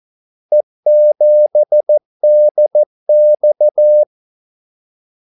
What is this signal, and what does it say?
Morse code, heard as a keyed tone, about 14 wpm, 600 Hz. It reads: E7DX
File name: Callsigns-14wpm